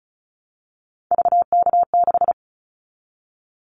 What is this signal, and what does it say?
Morse code, heard as a keyed tone, about 36 wpm, 700 Hz. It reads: VK6
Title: Callsign Prefixes for Top 100 DXCC - 35wpm